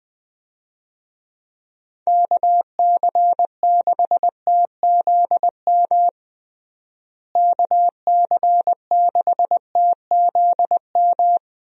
Morse code, heard as a keyed tone, about 20 wpm, 700 Hz. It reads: KC6TZM KC6TZM
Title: USA Call Signs - 6 Characters - Repeated 2x - 20wpm